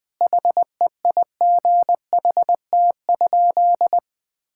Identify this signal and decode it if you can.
Morse code, heard as a keyed tone, about 21 wpm, 700 Hz. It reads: HEIGHT?
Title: Sentences from Top 1600 Words 20wpm